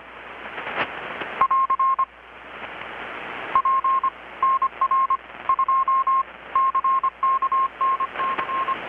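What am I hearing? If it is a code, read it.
Morse code, heard as a keyed tone, about 25 wpm, 1070 Hz. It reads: <AR> P NR 2 CK NC
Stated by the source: Morse from KPH's Night of Nights XX maritime radiotelegraphy event, received in Chiapas, Mexico about 3,800 km away on a Kenwood R-600 with a grounded 10 m long-wire antenna.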